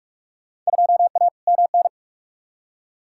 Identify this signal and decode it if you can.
Morse code, heard as a keyed tone, about 45 wpm, 700 Hz. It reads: 2A MN
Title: Field Day Exchanges - 45wpm